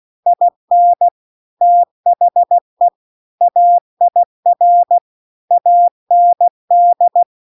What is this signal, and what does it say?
Morse code, heard as a keyed tone, about 16 wpm, 700 Hz. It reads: IN THE AIR AND